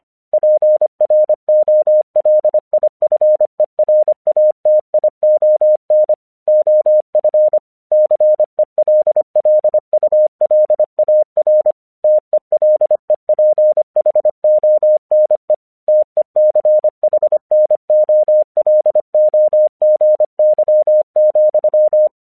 Morse code, heard as a keyed tone, about 25 wpm, 610 Hz. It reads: PROLIFERATION OF CELLULAR TELEPHONE TECHNOLOGY,
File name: 1 English 25wpm / AQC06